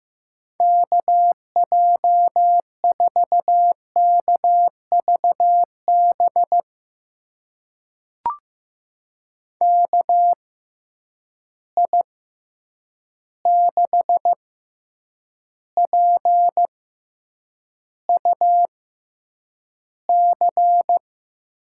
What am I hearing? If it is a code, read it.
Morse code, heard as a keyed tone, about 15 wpm, spaced as for 8 wpm, 700 Hz. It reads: KJ4KVB KI6PUC
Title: USA Call Signs - 6 Characters - 6x Character Spacing - 15wpm